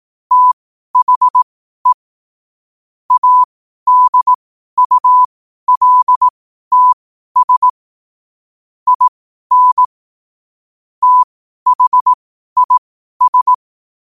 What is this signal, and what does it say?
Morse code, heard as a keyed tone, about 18 wpm, spaced as for 12 wpm, 1000 Hz. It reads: THE ADULTS IN THIS
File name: qotdfiles / 12wpm